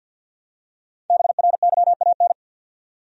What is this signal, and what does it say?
Morse code, heard as a keyed tone, about 52 wpm, 700 Hz. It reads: BRYAN
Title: USA Common Names - 50wpm